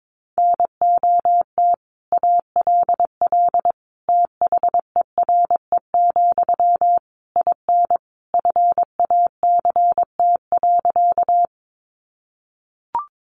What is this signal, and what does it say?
Morse code, heard as a keyed tone, about 22 wpm, 700 Hz. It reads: NOT ALL THERE, IN FACT.
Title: Sentences from Top 600 Words 22wpm